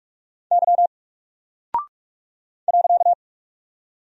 Morse code, heard as a keyed tone, about 44 wpm, 700 Hz. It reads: Y .
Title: Single Character (Letter:Number:Punctuation:Prosign) Alt 45wpm